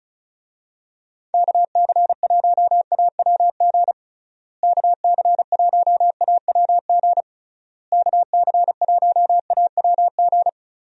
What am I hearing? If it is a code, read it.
Morse code, heard as a keyed tone, about 36 wpm, 700 Hz. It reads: KC1AWG KC1AWG KC1AWG
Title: USA Call Signs - 6 Characters - Repeated 3x - 35wpm